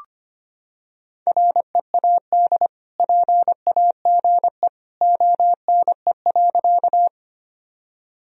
Morse code, heard as a keyed tone, about 25 wpm, 700 Hz. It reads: READ PAGE ONE.
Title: Sentences from Top 1000 Words - Review Part 1 - 25wpm